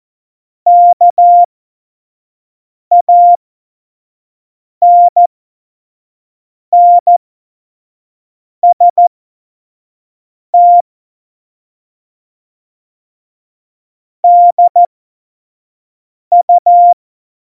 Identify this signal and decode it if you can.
Morse code, heard as a keyed tone, about 14 wpm, spaced as for 5 wpm, 700 Hz. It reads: KANNST DU
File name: Morsekurs / Witze / Witz 02. April 2019, 25 F70